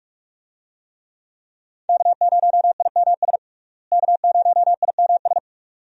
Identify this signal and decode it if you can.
Morse code, heard as a keyed tone, about 42 wpm, 700 Hz. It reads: K0IMS K0IMS
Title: USA Call Signs - 5 Characters - Repeated 2x - 45wpm